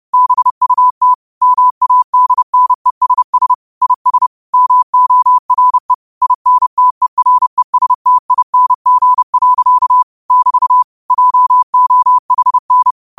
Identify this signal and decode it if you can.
Morse code, heard as a keyed tone, about 30 wpm, 1000 Hz. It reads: BUT MADNESS IS MORE INTERESTING. <BT> JOHN